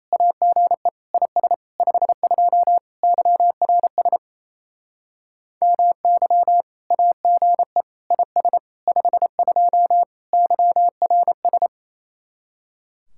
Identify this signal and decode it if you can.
Morse code, heard as a keyed tone, about 36 wpm, 700 Hz. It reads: AGE IS 52 YRS MY AGE IS 52 YRS